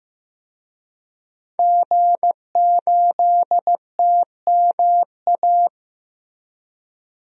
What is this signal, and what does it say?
Morse code, heard as a keyed tone, about 15 wpm, 700 Hz. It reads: G8TMA
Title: DXCC-100 Call Signs - 5 Characters - Repeated 2x - 15wpm